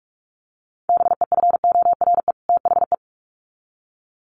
Morse code, heard as a keyed tone, about 44 wpm, 700 Hz. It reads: BEFORE THE
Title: Sets of 2 Words 45wpm